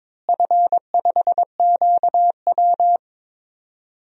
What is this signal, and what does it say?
Morse code, heard as a keyed tone, about 22 wpm, 700 Hz. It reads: F5QW
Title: DXCC-100 Call Signs - 4 Characters - Repeated 3x - 22wpm